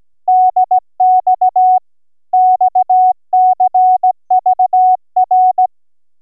Morse code, heard as a keyed tone, about 16 wpm, 740 Hz. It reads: DX XCVR